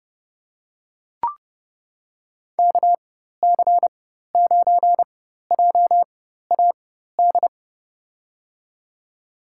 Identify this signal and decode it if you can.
Morse code, heard as a keyed tone, about 30 wpm, spaced as for 14 wpm, 700 Hz. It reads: KC9JAD
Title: USA Call Signs - 6 Characters - 4x Character Spacing - 30wpm